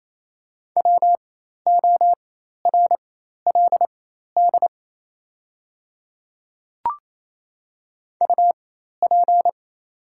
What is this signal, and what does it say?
Morse code, heard as a keyed tone, about 28 wpm, spaced as for 13 wpm, 690 Hz. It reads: WORLD UP
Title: 4x Character Spacing - Top 200 Words - 28wpm